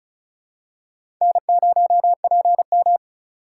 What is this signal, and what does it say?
Morse code, heard as a keyed tone, about 34 wpm, 700 Hz. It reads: N0PM
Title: USA Call Signs - 4x Character Spacing - 35wpm